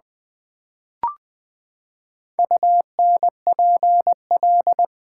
Morse code, heard as a keyed tone, about 20 wpm, 700 Hz. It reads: UNPL
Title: Build A Word - 4 Letters At A Time - 10-Letter Words - 20wpm